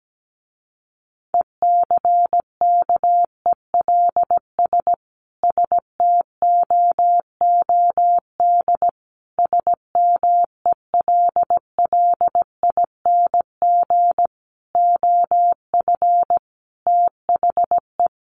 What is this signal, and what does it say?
Morse code, heard as a keyed tone, about 17 wpm, 700 Hz. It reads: ECKELS STOOD SMELLING OF THE
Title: A Sound Of Thunder 17wpm